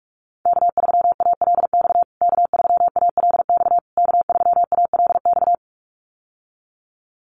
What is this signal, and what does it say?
Morse code, heard as a keyed tone, about 44 wpm, 690 Hz. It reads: K3ALX K3ALX K3ALX
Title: USA Call Signs Repeated 3x 45wpm